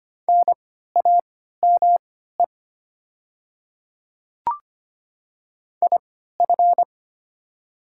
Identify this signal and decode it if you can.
Morse code, heard as a keyed tone, about 25 wpm, spaced as for 14 wpm, 700 Hz. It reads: NAME IF
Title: 3x Character Spacing - Top 200 Words - 25wpm